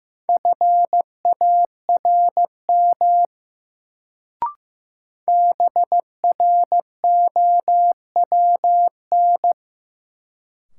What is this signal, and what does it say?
Morse code, heard as a keyed tone, about 15 wpm, 700 Hz. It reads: FARM BROWN
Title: Sentences from Top 1200 Words - New Word Drill Alt - 15wpm